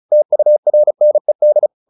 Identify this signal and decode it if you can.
Morse code, heard as a keyed tone, about 34 wpm, 600 Hz. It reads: TURNED
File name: the old man and the sea / the old man and the sea 1 35wpm